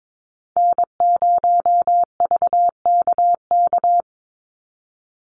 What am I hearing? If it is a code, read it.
Morse code, heard as a keyed tone, about 22 wpm, 700 Hz. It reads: N0VKK
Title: USA Call Signs 22wpm